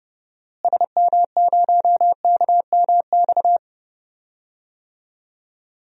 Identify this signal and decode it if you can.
Morse code, heard as a keyed tone, about 31 wpm, 700 Hz. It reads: SM0KMX
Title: DXCC-100 Call Signs - 6 Characters - Repeated 2x - 30wpm